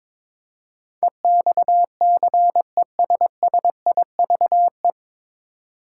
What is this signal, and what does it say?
Morse code, heard as a keyed tone, about 22 wpm, 700 Hz. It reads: EXCESSIVE
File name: 9 Letter Words - First 3 Letters Given - 22wpm